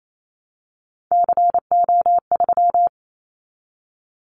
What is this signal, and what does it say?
Morse code, heard as a keyed tone, about 28 wpm, 700 Hz. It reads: CO3
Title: Callsign Prefixes for Top 100 DXCC - 28wpm